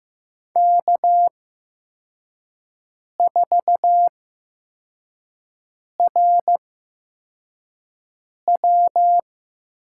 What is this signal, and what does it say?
Morse code, heard as a keyed tone, about 15 wpm, spaced as for 4 wpm, 700 Hz. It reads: K4RW
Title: USA Call Signs - 4 Characters - 8x Character Spacing - 15wpm